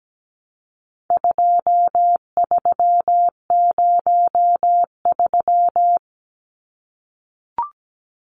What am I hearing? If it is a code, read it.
Morse code, heard as a keyed tone, about 17 wpm, 700 Hz. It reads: 2303
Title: Numbers - 4-Digit 17wpm